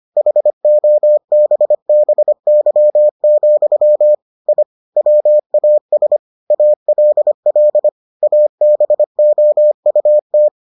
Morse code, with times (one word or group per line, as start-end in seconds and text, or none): 0.2-4.2 HOBBY,
4.5-4.6 I
5.0-6.2 WAS
6.5-7.9 ALL
8.2-10.5 ABOUT